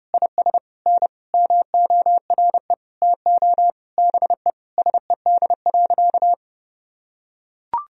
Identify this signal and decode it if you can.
Morse code, heard as a keyed tone, about 32 wpm, 700 Hz. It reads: IS N MORE TO BE SED.
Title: Sentences from Top 100 Words and Abbreviations 30wpm